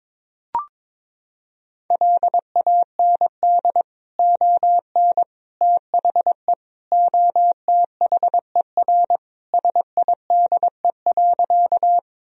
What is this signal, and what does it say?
Morse code, heard as a keyed tone, about 21 wpm, 700 Hz. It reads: LAND ON THE OTHER SIDE.
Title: Sentences from Top 1000 Words - Review Part 1 - 22wpm